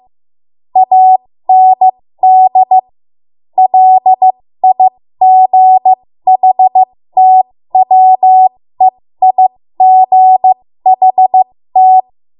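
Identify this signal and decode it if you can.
Morse code, heard as a keyed tone, about 15 wpm, spaced as for 13 wpm, 750 Hz. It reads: AND LIGHTWEIGHT
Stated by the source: ARRL Morse code practice at 13 WPM.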